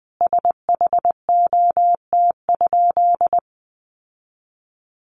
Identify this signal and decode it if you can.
Morse code, heard as a keyed tone, about 20 wpm, 700 Hz. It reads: SHOT?
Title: Sentences from Top 1100 Words 20wpm